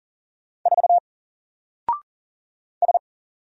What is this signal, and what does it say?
Morse code, heard as a keyed tone, about 39 wpm, 700 Hz. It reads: 4 S
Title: Single Character (Letter:Number:Punctuation:Prosign) 40wpm